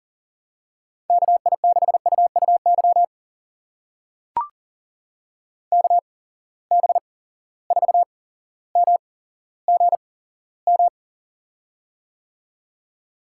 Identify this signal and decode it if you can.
Morse code, heard as a keyed tone, about 42 wpm, spaced as for 17 wpm, 700 Hz. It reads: KI6UUY KB4MGM
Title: USA Call Signs - 6 Characters - 8x Character Spacing - 40wpm